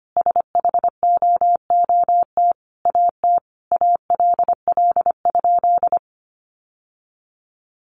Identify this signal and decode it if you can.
Morse code, heard as a keyed tone, about 25 wpm, 700 Hz. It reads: SHOOT AT ALL?
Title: Sentences from Top 1200 Words 25wpm